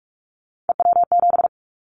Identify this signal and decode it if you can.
Morse code, heard as a keyed tone, about 46 wpm, 700 Hz. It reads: EW7
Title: Callsign Prefixes for Top 50 DXCC - 45wpm